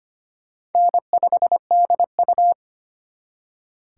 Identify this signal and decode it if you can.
Morse code, heard as a keyed tone, about 32 wpm, 700 Hz. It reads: N5DU